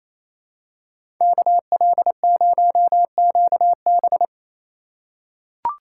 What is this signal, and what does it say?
Morse code, heard as a keyed tone, about 28 wpm, 700 Hz. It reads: KL0QB
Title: USA Call Signs - 5 Characters - 4x Character Spacing - 28wpm